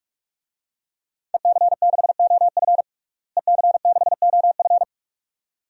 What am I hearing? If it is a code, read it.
Morse code, heard as a keyed tone, about 41 wpm, 700 Hz. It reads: EC6OF EC6OF
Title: DXCC-100 Call Signs - 5 Characters - Repeated 2x - 45wpm